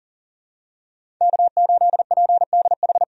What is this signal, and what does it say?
Morse code, heard as a keyed tone, about 40 wpm, 700 Hz. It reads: K8PDH
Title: USA Call Signs - 5 Characters - 40wpm